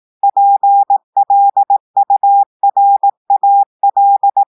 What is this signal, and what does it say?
Morse code, heard as a keyed tone, about 18 wpm, 800 Hz. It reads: PLURAL
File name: audios / portugais / portugais texte en clair 1 vitesse 18 mots minutes